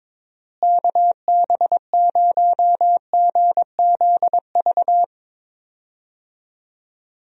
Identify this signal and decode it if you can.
Morse code, heard as a keyed tone, about 22 wpm, 700 Hz. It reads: KB0GZV
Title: USA Call Signs - 6 Characters - Repeated 3x - 22wpm